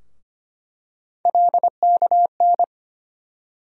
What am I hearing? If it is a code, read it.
Morse code, heard as a keyed tone, about 25 wpm, 700 Hz. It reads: LKN
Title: USA Call Sign Postfixes - 3 Letters 25wpm